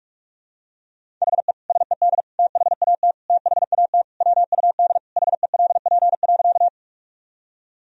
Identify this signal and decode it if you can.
Morse code, heard as a keyed tone, about 46 wpm, 700 Hz. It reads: HE SED THAT THAT WUD HELP.
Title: Sentences from Top 200 Words and Abbreviations 45wpm